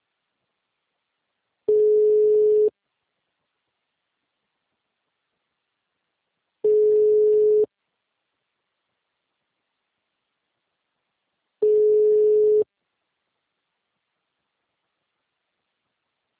Ringback tone (Europe, 425 Hz).
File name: out- -2008- - - .21424